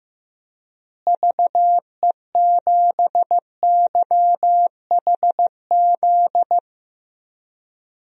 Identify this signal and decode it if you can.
Morse code, heard as a keyed tone, about 15 wpm, 700 Hz. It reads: VE7YHZ
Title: DXCC-100 Call Signs - 6 Characters - Repeated 3x - 15wpm